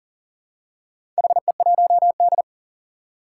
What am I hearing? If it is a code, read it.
Morse code, heard as a keyed tone, about 40 wpm, 700 Hz. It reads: HE1D
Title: Four Challenging Letter-Number Combinations 40wpm